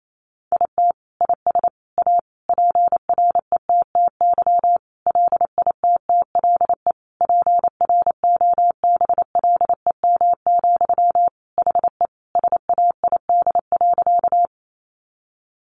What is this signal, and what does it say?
Morse code, heard as a keyed tone, about 28 wpm, 700 Hz. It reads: IT IS A PRETTY LITTLE PROBLEM, HE SAID.
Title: Sentences from Top 1000 Words 28wpm